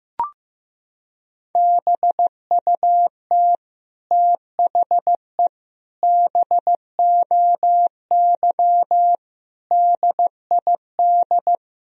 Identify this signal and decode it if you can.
Morse code, heard as a keyed tone, about 15 wpm, 700 Hz. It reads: BUT THE BOY DID
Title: Sentences from Top 1000 Words - Review Part 3 - 15wpm